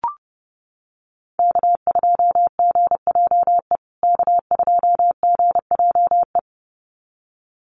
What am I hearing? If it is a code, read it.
Morse code, heard as a keyed tone, about 30 wpm, 700 Hz. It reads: K2GJE K2GJE
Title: USA Call Signs Repeated 2x 30wpm